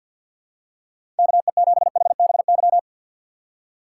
Morse code, heard as a keyed tone, about 51 wpm, 700 Hz. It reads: KE7HBY
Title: USA Call Signs - 6 Characters - 3x Character Spacing - 50wpm